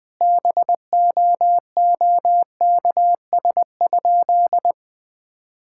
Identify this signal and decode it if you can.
Morse code, heard as a keyed tone, about 20 wpm, 700 Hz. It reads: BOOKS?
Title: Sentences with 9-Word Phrases - Spoken Beginning - 20wpm